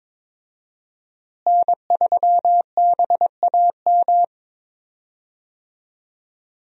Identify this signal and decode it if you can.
Morse code, heard as a keyed tone, about 22 wpm, 700 Hz. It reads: N3BAM
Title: USA Call Signs - 5 Characters - Repeated 2x - 22wpm